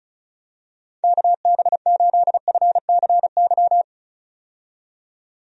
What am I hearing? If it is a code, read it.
Morse code, heard as a keyed tone, about 34 wpm, 700 Hz. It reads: KB8FCY
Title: USA Call Signs - 6 Characters - 35wpm